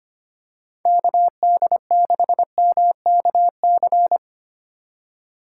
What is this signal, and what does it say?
Morse code, heard as a keyed tone, about 25 wpm, 700 Hz. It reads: KD6MKC
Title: USA Call Signs - 6 Characters - 4x Character Spacing - 25wpm